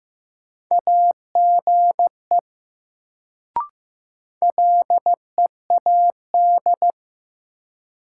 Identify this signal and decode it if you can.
Morse code, heard as a keyed tone, about 15 wpm, 700 Hz. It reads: AGE LEAD
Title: Sentences from Top 500 Words - New Word Drill - 15wpm